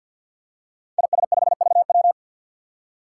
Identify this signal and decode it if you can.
Morse code, heard as a keyed tone, about 52 wpm, 700 Hz. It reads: IS5VW
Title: Five Challenging Letter-Number Combinations 50wpm